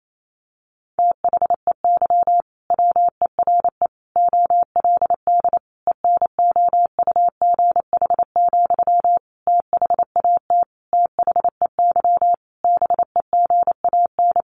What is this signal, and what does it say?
Morse code, heard as a keyed tone, about 28 wpm, 700 Hz. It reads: THEY WERE OLD ENOUGH, THAT THEY BEGAN
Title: Sentences from Top 800 Words 28wpm